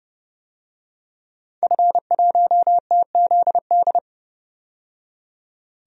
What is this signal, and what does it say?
Morse code, heard as a keyed tone, about 30 wpm, 700 Hz. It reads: F1TZD
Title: DXCC-100 Call Signs - 5 Characters - 30wpm